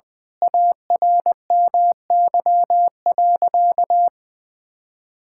Morse code, heard as a keyed tone, about 21 wpm, 700 Hz. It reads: ARMY.
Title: Sentences from Top 1000 Words - Review Part 6 - 20wpm